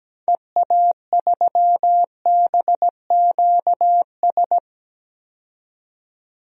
Morse code, heard as a keyed tone, about 17 wpm, 700 Hz. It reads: EA3BQS
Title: DXCC-100 Call Signs - 6 Characters - 17wpm